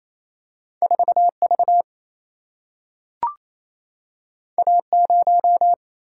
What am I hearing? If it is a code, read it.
Morse code, heard as a keyed tone, about 28 wpm, 700 Hz. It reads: 4 V A 0
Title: Single-Letter Single-Number 28wpm